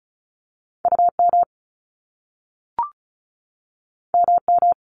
Morse code, heard as a keyed tone, about 36 wpm, 700 Hz. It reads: UM MM
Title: Top Two Letter Combinations in Words 35wpm